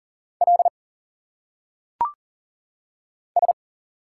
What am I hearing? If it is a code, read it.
Morse code, heard as a keyed tone, about 40 wpm, 700 Hz. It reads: L S
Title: Single Letters - Phonetic Pronunciation 40wpm